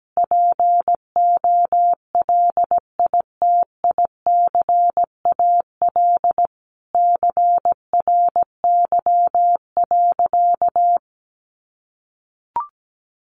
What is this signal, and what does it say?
Morse code, heard as a keyed tone, about 17 wpm, 700 Hz. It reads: POLITICAL CRY.
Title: Sentences from Top 1400 Words 17wpm